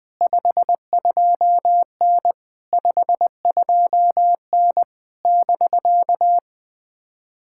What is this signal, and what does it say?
Morse code, heard as a keyed tone, about 21 wpm, 700 Hz. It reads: 52N 52N <BK>
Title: POTA QSO Practice - 20wpm